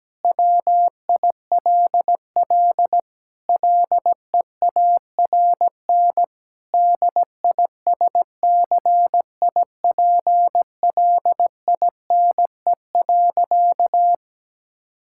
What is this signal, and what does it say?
Morse code, heard as a keyed tone, about 17 wpm, 700 Hz. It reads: WILL LEARN DISCIPLINE.
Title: Sentences from Top 1000 Words - Review Part 6 - 17wpm